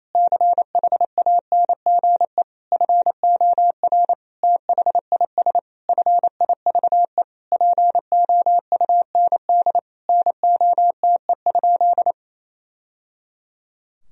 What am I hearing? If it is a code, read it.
Morse code, heard as a keyed tone, about 28 wpm, 700 Hz. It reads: CHANGE FOR THIS FIVE POUND NOTE?
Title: Sentences from Top 1000 Words - Review Part 2 - 28wpm